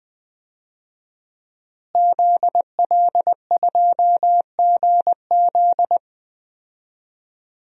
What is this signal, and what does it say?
Morse code, heard as a keyed tone, about 20 wpm, 700 Hz. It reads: ZL2GZ
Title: DXCC-100 Call Signs - 5 Characters - Repeated 3x - 20wpm